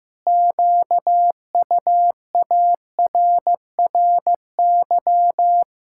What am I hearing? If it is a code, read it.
Morse code, heard as a keyed tone, about 16 wpm, 700 Hz. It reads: QUARRY
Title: Pangram Challenge v2 - 15wpm